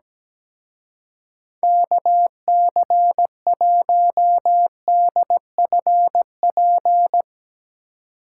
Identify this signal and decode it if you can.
Morse code, heard as a keyed tone, about 18 wpm, 700 Hz. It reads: KC1DFP
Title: Speed Racing - USA 2x3 Call Signs 17wpm